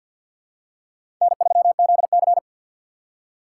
Morse code, heard as a keyed tone, about 47 wpm, 700 Hz. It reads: N3ZC
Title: USA Call Signs - 4 Characters - 3x Character Spacing - 50wpm